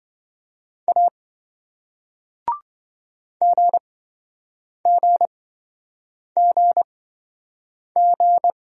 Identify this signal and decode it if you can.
Morse code, heard as a keyed tone, about 28 wpm, spaced as for 7 wpm, 700 Hz. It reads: A GGGG